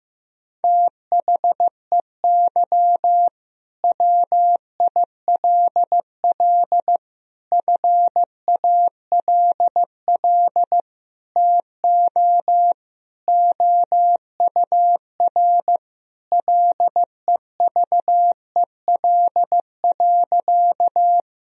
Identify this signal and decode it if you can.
Morse code, heard as a keyed tone, about 15 wpm, 700 Hz. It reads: THEY WILL FALL TO OUR LEVEL.
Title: Sentences from Top 1000 Words - Review Part 4 - 15wpm